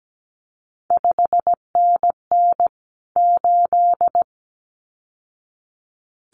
Morse code, heard as a keyed tone, about 17 wpm, 700 Hz. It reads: 5NN 8
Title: CQ WW DX Contest Exchange - 17wpm